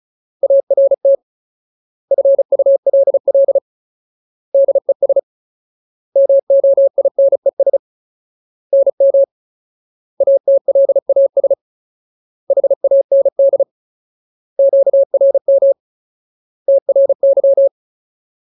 Morse code, heard as a keyed tone, about 34 wpm, 550 Hz. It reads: ART FULL DES MOINES NM ATLAS HAND QRM TRY
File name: Ultimate Word List / 35WPM Ultimate